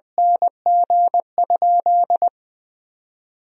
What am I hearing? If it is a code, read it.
Morse code, heard as a keyed tone, about 21 wpm, 700 Hz. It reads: NG?
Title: 4x Word-Spacing - Sentences from Top 200 Words - 20wpm